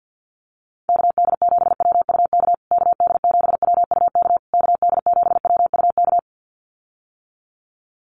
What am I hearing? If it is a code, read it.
Morse code, heard as a keyed tone, about 50 wpm, 700 Hz. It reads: KD7WUK KD7WUK KD7WUK
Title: USA Call Signs Repeated 3x 50wpm